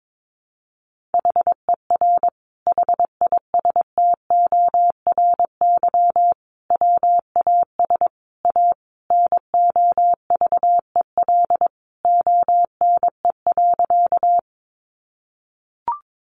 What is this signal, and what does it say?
Morse code, heard as a keyed tone, about 22 wpm, 700 Hz. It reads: HER HISTORY WAS A NOVEL ONE.
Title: Sentences from Top 1300 Words 22wpm